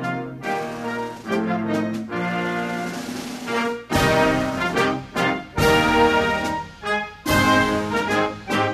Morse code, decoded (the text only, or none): none